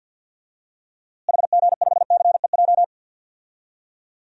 Morse code, heard as a keyed tone, about 47 wpm, 700 Hz. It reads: HG5CEJ
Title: DXCC-100 Call Signs - 6 Characters - Repeated 3x - 50wpm